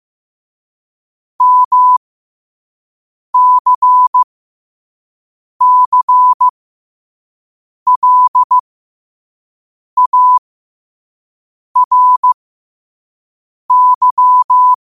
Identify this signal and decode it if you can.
Morse code, heard as a keyed tone, about 15 wpm, spaced as for 5 wpm, 1000 Hz. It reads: MCCLARY
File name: 5wpm